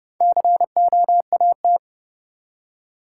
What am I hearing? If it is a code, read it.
Morse code, heard as a keyed tone, about 30 wpm, 700 Hz. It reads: COAT
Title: Four Letter Words - 30wpm